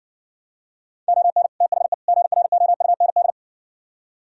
Morse code, heard as a keyed tone, about 60 wpm, 700 Hz. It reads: ON THE GROUND